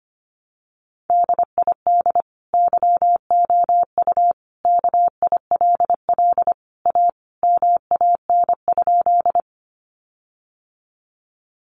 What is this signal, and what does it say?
Morse code, heard as a keyed tone, about 25 wpm, 700 Hz. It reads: DID YOU KILL A MAN?
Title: Sentences from Top 1400 Words 25wpm